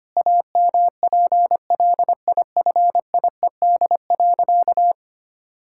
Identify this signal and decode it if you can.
Morse code, heard as a keyed tone, about 25 wpm, 700 Hz. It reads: AMPLIFIED.
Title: Pangram Challenge v2 - 25wpm